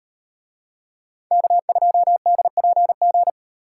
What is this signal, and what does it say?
Morse code, heard as a keyed tone, about 36 wpm, 700 Hz. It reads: K2DPG